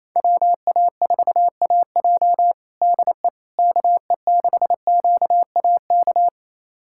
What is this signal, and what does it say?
Morse code, heard as a keyed tone, about 27 wpm, 700 Hz. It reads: WA4AJ DE KE6QAK
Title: USA Callsign DE Callsign 28wpm